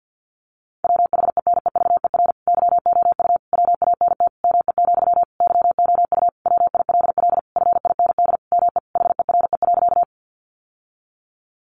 Morse code, heard as a keyed tone, about 50 wpm, 700 Hz. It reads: WHENEVER YOU WANT ME, YOU WILL FIND ME HERE.
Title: Sentences from Top 1200 Words 50wpm